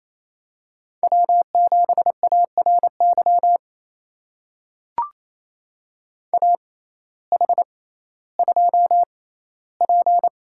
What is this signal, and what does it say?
Morse code, heard as a keyed tone, about 28 wpm, spaced as for 15 wpm, 700 Hz. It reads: W7ARY AH2P
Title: USA Call Signs - 6x Character Spacing - 28wpm